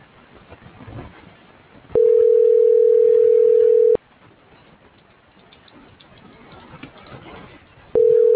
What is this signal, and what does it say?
Ringback tone (North America), 440 + 480 Hz.